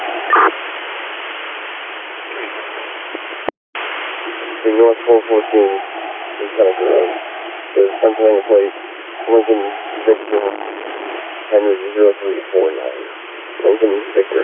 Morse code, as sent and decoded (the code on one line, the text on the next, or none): none